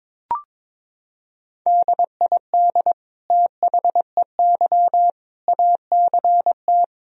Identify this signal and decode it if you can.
Morse code, heard as a keyed tone, about 22 wpm, 700 Hz. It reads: DID THEY ACT
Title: Sentences from Top 1000 Words - Review Part 4 - 22wpm